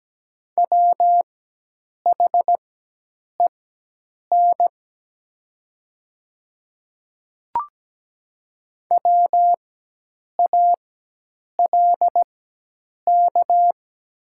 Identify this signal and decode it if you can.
Morse code, heard as a keyed tone, about 17 wpm, spaced as for 8 wpm, 700 Hz. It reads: WHEN WALK
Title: 4x Character Spacing - Top 300 Words - 17wpm